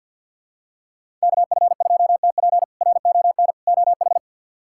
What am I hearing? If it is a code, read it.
Morse code, heard as a keyed tone, about 50 wpm, 700 Hz. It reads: KF2TP RON OH
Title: K1USN SST QSO Practice - 50wpm